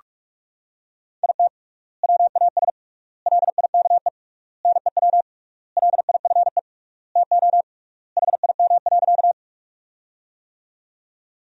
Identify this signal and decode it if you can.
Morse code, heard as a keyed tone, about 42 wpm, 700 Hz. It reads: IT WAS LIKE NEW LIFE TO HIM.
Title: 3x Word-Spacing - Sentences from Top 300 Words - 45wpm